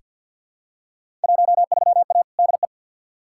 Morse code, heard as a keyed tone, about 48 wpm, 700 Hz. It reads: 13A DE
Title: Field Day Exchanges Encore - 50wpm